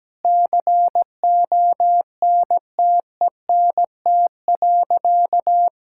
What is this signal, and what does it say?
Morse code, heard as a keyed tone, about 17 wpm, 700 Hz. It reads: CONTENT.
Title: Sentences from Top 1000 Words - Review Part 6 - 17wpm